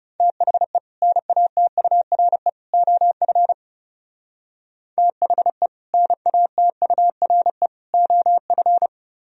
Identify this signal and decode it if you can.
Morse code, heard as a keyed tone, about 37 wpm, 700 Hz. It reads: THE NATURE OF THE NATURE OF
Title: Speed-Racing - Sets of 3 Words 30wpm